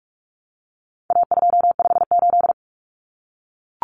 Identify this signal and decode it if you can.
Morse code, heard as a keyed tone, about 44 wpm, 700 Hz. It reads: A258
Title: Common Cut Numbers - 4-Digit 45wpm